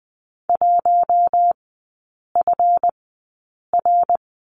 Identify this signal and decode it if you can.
Morse code, heard as a keyed tone, about 20 wpm, spaced as for 8 wpm, 700 Hz. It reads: 1FR
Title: 172 Lesson 28e - Number 9 - Three Characters 20wpm